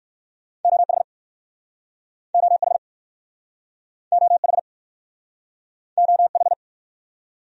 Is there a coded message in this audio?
Yes: Morse code from 0.6 to 6.5 s, heard as a keyed tone, about 61 wpm, 700 Hz. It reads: OH OH OH OH